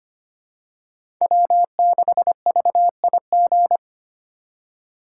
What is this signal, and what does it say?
Morse code, heard as a keyed tone, about 25 wpm, 700 Hz. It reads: W6VIG
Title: USA Call Signs - 5 Characters - 2x Character Spacing - 25wpm